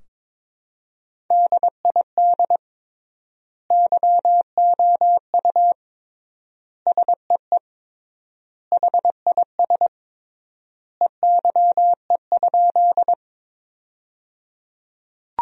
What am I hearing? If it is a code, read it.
Morse code, heard as a keyed tone, about 23 wpm, 700 Hz. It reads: DID YOU SEE HIS EYE?
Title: 3x Word-Spacing - Sentences from Top 300 Words - 22wpm